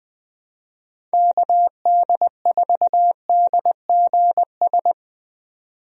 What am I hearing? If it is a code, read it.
Morse code, heard as a keyed tone, about 21 wpm, 700 Hz. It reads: KD4DGS
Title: USA Call Signs - 6 Characters - 20wpm